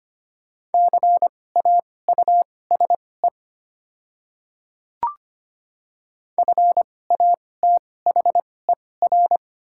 Morse code, heard as a keyed tone, about 25 wpm, spaced as for 18 wpm, 690 Hz. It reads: CAUSE FATHER
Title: 2x Character Spacing - Top 200 Words - 25wpm